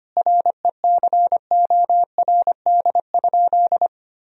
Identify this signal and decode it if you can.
Morse code, heard as a keyed tone, about 25 wpm, 700 Hz. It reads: RECORD?
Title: Sentences from Top 1000 Words - Review Part 4 - 25wpm